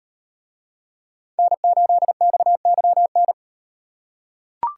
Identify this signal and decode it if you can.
Morse code, heard as a keyed tone, about 38 wpm, 700 Hz. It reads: N8XYN